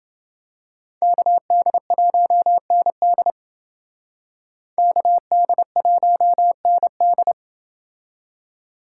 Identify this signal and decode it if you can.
Morse code, heard as a keyed tone, about 29 wpm, 700 Hz. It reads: KD1ND KD1ND